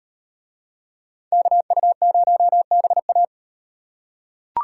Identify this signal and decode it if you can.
Morse code, heard as a keyed tone, about 38 wpm, 700 Hz. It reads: KU0BA